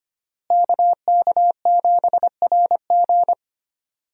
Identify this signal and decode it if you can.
Morse code, heard as a keyed tone, about 25 wpm, 700 Hz. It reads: KK7RG
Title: USA Call Signs - 5 Characters - Repeated 3x - 25wpm